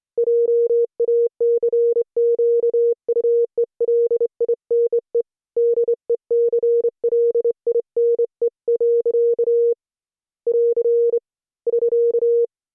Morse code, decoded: JACQUELINE DECLINE. <AR> <SK>